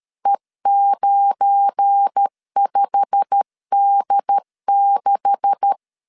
Morse code, heard as a keyed tone, about 13 wpm, 800 Hz. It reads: E95D6
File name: LEZIONI / 9 286 12wpm 2305